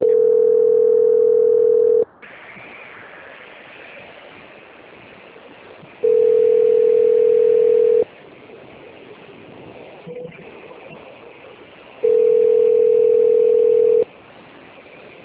Ringback tone (North America), 440 + 480 Hz.